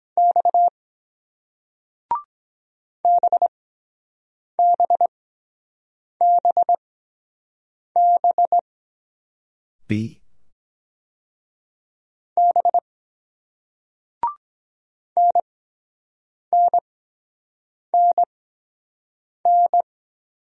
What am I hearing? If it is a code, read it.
Morse code, heard as a keyed tone, about 24 wpm, spaced as for 7 wpm, 700 Hz. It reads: X BBBB B NNNN